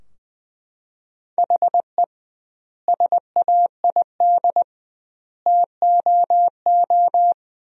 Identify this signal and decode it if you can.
Morse code, heard as a keyed tone, about 20 wpm, 700 Hz. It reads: HE SAID TOO
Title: 2x Word-Spacing - Sentences from Top 200 Words - 20wpm